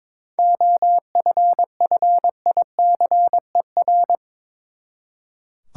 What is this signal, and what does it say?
Morse code, heard as a keyed tone, about 22 wpm, 700 Hz. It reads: OFFICER
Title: Seven Letter Words - 22wpm